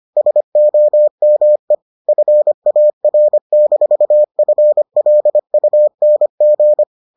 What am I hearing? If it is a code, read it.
Morse code, heard as a keyed tone, about 25 wpm, 600 Hz. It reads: SOME FAR-FLUNG